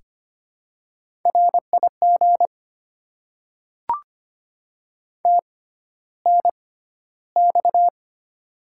Morse code, heard as a keyed tone, about 25 wpm, spaced as for 13 wpm, 700 Hz. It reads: RIG TNX
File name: Top 20 QSO Elements - 6x Character Spacing - 25wpm